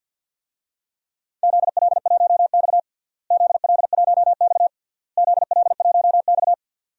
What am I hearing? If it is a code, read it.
Morse code, heard as a keyed tone, about 50 wpm, 700 Hz. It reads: ZL1X ZL1X ZL1X
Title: DXCC-100 Call Signs - 4 Characters - Repeated 3x - 50wpm